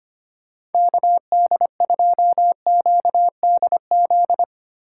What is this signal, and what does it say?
Morse code, heard as a keyed tone, about 33 wpm, 700 Hz. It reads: KD2QDZ